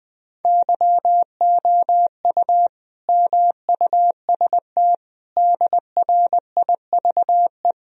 Morse code, heard as a keyed tone, about 20 wpm, 700 Hz. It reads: YOU MUST DRIVE
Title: Sentences from Top 1000 Words - Review Part 2 - 20wpm